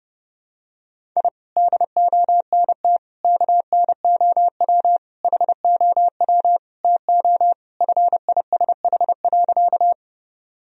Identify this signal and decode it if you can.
Morse code, heard as a keyed tone, about 30 wpm, 700 Hz. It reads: I DONT KNOW HOW TO FISH.
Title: Sentences from Top 1000 Words - Review Part 1 - 30wpm